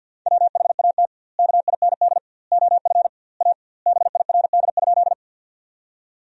Morse code, heard as a keyed tone, about 50 wpm, 700 Hz. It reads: WHAT KIND OF A BIRD?
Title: 2x Word-Spacing - Sentences from Top 400 Words - 50wpm